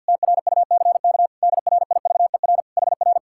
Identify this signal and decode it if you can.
Morse code, heard as a keyed tone, about 65 wpm, 700 Hz. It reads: TRUCK DRIVER HR